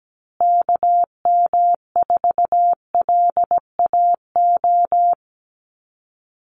Morse code, heard as a keyed tone, about 17 wpm, 700 Hz. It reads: KM4LAO
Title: Call Signs Repeated 3x - 17wpm